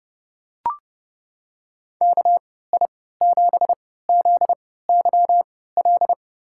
Morse code, heard as a keyed tone, about 32 wpm, spaced as for 17 wpm, 700 Hz. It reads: KI7ZYL
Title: USA Call Signs - 3x Character Spacing - 30wpm